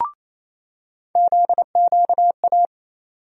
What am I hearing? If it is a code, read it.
Morse code, heard as a keyed tone, about 28 wpm, 700 Hz. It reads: ZQA
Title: USA Call Sign Postfixes - 3 Letters 28wpm